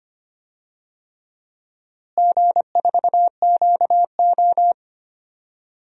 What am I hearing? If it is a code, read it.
Morse code, heard as a keyed tone, about 25 wpm, 700 Hz. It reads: G4QO
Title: DXCC-100 Call Signs - 4 Characters - Repeated 3x - 25wpm